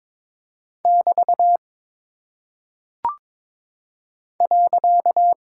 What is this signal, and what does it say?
Morse code, heard as a keyed tone, about 22 wpm, 700 Hz. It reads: <BT> .
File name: practice / 22wpm / Single Punctuation-Prosign - 22wpm